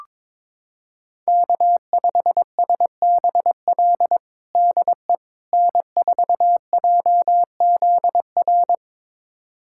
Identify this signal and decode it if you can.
Morse code, heard as a keyed tone, about 21 wpm, 700 Hz. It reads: K5SBL DE N4JZR
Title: USA Callsign DE Callsign - 5 Characters - 22wpm